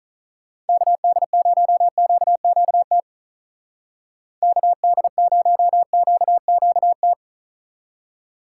Morse code, heard as a keyed tone, about 43 wpm, 700 Hz. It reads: KD0QQT KD0QQT
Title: Speed Racing - USA 2x3 Call Signs 35wpm